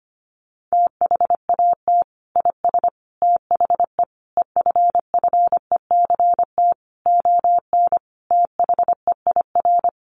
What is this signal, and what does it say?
Morse code, heard as a keyed tone, about 25 wpm, 700 Hz. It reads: THAT IS THE EFFECT ON THEIR
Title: Sentences from Top 1300 Words 25wpm